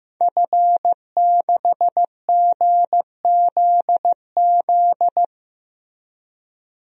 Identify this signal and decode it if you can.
Morse code, heard as a keyed tone, about 15 wpm, 700 Hz. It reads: F6GZZ
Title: DXCC-100 Call Signs - 5 Characters - Repeated 3x - 15wpm